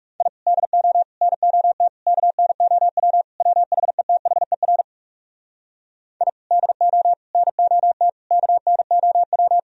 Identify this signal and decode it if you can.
Morse code, heard as a keyed tone, about 40 wpm, 700 Hz. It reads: I DO NOT KNOW WHETHER I DO NOT KNOW